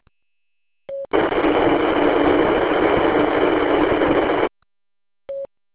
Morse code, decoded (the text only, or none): none